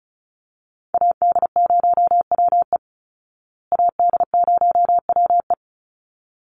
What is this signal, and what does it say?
Morse code, heard as a keyed tone, about 34 wpm, 700 Hz. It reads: AD0WE AD0WE
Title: Call Signs Repeated 3x - 35wpm